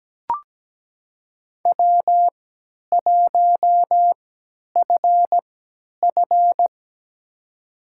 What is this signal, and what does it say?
Morse code, heard as a keyed tone, about 17 wpm, spaced as for 10 wpm, 700 Hz. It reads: W1FF
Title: USA Call Signs - 4 Characters - 3x Character Spacing - 17wpm